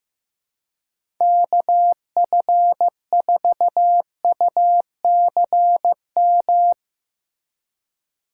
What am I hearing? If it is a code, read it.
Morse code, heard as a keyed tone, about 15 wpm, 700 Hz. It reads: KF4UCM
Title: USA Call Signs - 6 Characters - 4x Character Spacing - 15wpm